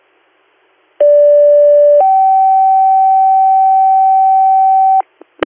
Two-tone page, heard as a one-second tone, then a three-second tone, about 580 then 770 Hz, with noise. It is heard from 1.0 to 5.0 s.